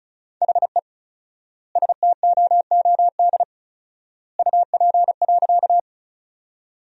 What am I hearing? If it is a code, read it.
Morse code, heard as a keyed tone, about 36 wpm, 700 Hz. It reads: HE STOOD UP.
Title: 4x Word-Spacing - Sentences from Top 500 Words - 35wpm